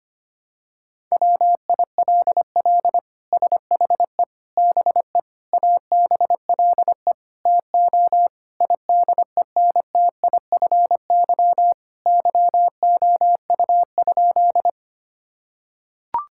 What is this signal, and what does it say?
Morse code, heard as a keyed tone, about 25 wpm, 700 Hz. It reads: WILL SHE BE ABLE TO IDENTIFY YOU?
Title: Sentences from Top 1000 Words - Review Part 6 - 25wpm